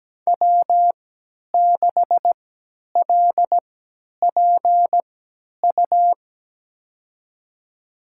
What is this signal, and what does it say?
Morse code, heard as a keyed tone, about 17 wpm, spaced as for 10 wpm, 700 Hz. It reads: W6LPU
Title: USA Call Signs - 3x Character Spacing - 17wpm